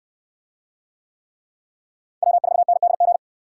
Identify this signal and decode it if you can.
Morse code, heard as a keyed tone, about 69 wpm, 700 Hz. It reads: W4NDG